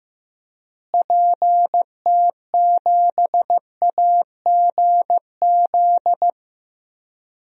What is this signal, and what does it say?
Morse code, heard as a keyed tone, about 15 wpm, 700 Hz. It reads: PT7AGZ
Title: DXCC-100 Call Signs - 6 Characters - Repeated 3x - 15wpm